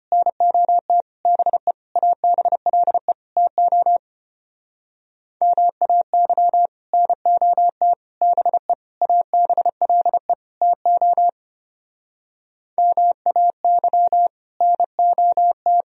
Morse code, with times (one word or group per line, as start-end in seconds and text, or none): 0.1-1.0 NOT
1.3-1.7 BE
2.0-3.1 ABLE
3.4-4.0 TO
5.4-6.7 MAY
6.9-7.9 NOT
8.2-8.7 BE
9.0-10.3 ABLE
10.6-11.3 TO
12.8-14.3 MAY
14.6-15.8 NOT